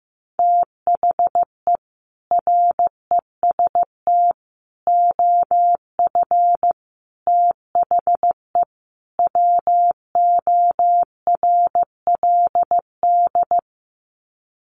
Morse code, heard as a keyed tone, about 15 wpm, 700 Hz. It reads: THE REST OF THE WORLD
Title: Sets of 5 Words 15wpm